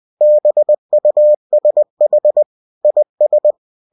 Morse code, heard as a keyed tone, about 20 wpm, 600 Hz. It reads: BUSH IS
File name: cw-books / winnie-the-pooh / winnie-20wpm-page-0045